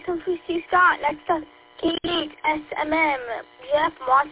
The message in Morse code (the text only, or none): none